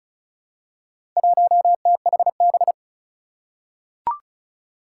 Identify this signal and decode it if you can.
Morse code, heard as a keyed tone, about 36 wpm, 700 Hz. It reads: 1THB
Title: Four Challenging Letter-Number Combinations 35wpm